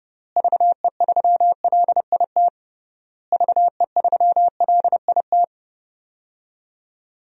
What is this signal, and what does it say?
Morse code, heard as a keyed tone, about 32 wpm, 700 Hz. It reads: VE3LIT VE3LIT
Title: DXCC-100 Call Signs - 6 Characters - Repeated 3x - 30wpm